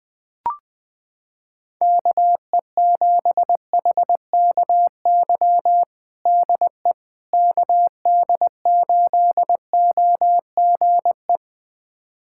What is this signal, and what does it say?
Morse code, heard as a keyed tone, about 21 wpm, 700 Hz. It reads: KE7HKY DE KD8OGE
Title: USA Callsign DE Callsign - 6 Characters - 20wpm